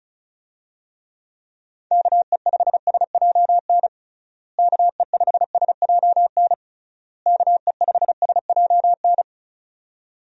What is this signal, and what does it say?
Morse code, heard as a keyed tone, about 36 wpm, 700 Hz. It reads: KE5SJN KE5SJN KE5SJN
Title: USA Call Signs - 6 Characters - Repeated 3x - 35wpm